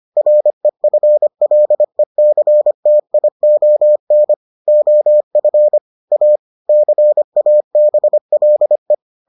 Morse code, heard as a keyed tone, about 25 wpm, 600 Hz. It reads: REFLECTION OF A CABLE